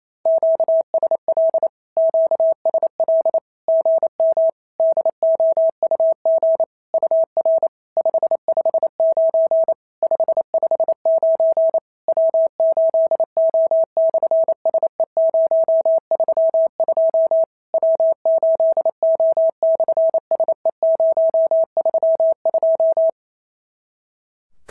Morse code, heard as a keyed tone, about 28 wpm, 650 Hz. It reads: QSL QSL GM DOUG UR 559 559 W8O/SE032 W8O/SE032
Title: SOTA QSO Practice 28wpm